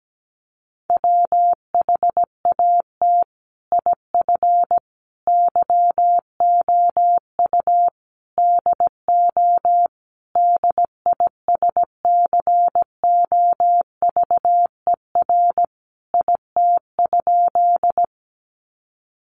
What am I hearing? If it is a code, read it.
Morse code, heard as a keyed tone, about 17 wpm, 700 Hz. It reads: WHAT IF YOU DO DISCOVER IT?
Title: Sentences from Top 1100 Words 17wpm